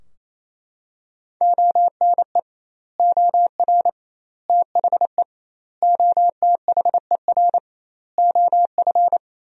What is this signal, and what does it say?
Morse code, heard as a keyed tone, about 28 wpm, 700 Hz. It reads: ONE OR THE OTHER OF
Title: 2x Word-Spacing - Sentences from Top 400 Words - 28wpm